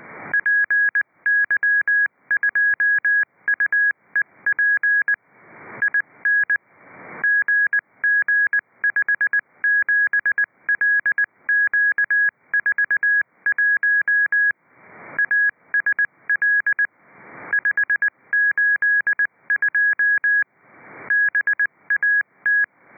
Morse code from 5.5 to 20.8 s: IN GG57LQ41 ASL 582